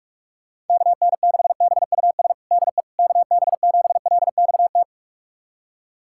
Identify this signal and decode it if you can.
Morse code, heard as a keyed tone, about 46 wpm, 700 Hz. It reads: KN6BUS DE KB7LXT
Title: USA Callsign DE Callsign - 6 Characters - 45wpm